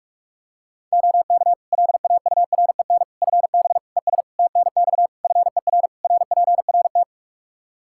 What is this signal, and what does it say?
Morse code, heard as a keyed tone, about 46 wpm, 700 Hz. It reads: OK LAUREN FB ES TNX FER RPRT
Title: Practice QSO - Exchange 2 - Info - 45wpm